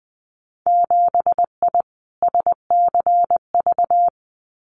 Morse code, heard as a keyed tone, about 20 wpm, 700 Hz. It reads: 7I SCV
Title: Winter Field Day Exchanges Encore 20wpm